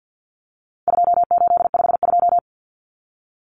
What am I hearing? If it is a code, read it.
Morse code, heard as a keyed tone, about 50 wpm, 700 Hz. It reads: 2852
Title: Common Cut Numbers - 4-Digit 50wpm